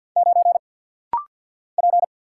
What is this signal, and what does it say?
Morse code, heard as a keyed tone, about 50 wpm, 700 Hz. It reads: 9 P
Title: Single Letter-Number Alt - Rapid-Fire 50wpm